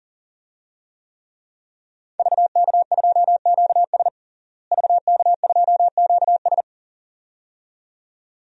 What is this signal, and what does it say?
Morse code, heard as a keyed tone, about 40 wpm, 690 Hz. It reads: VK2QS VK2QS
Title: DXCC-100 Call Signs - 5 Characters - Repeated 2x - 40wpm